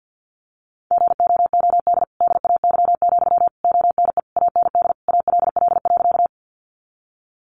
Morse code, heard as a keyed tone, about 50 wpm, 700 Hz. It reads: GOOD DAY, ONE AND ALL.
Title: Sentences from Top 200 Words 50wpm